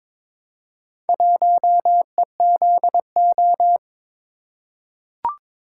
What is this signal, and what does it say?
Morse code, heard as a keyed tone, about 22 wpm, 690 Hz. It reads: 1EZO
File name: practice / 22wpm / Four Letter-Number Combinations - 22wpm